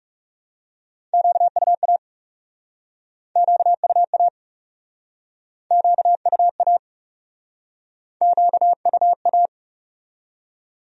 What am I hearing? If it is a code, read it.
Morse code, heard as a keyed tone, about 42 wpm, 700 Hz. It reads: QUA QUA QUA QUA